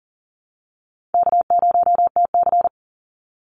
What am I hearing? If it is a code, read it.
Morse code, heard as a keyed tone, about 40 wpm, 700 Hz. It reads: K0TC
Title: Call Signs Repeated 2x - 40wpm